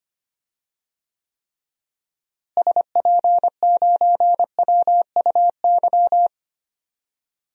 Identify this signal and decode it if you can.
Morse code, heard as a keyed tone, about 25 wpm, 700 Hz. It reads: SP9WUY
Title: DXCC-100 Call Signs - 6 Characters - Repeated 2x - 25wpm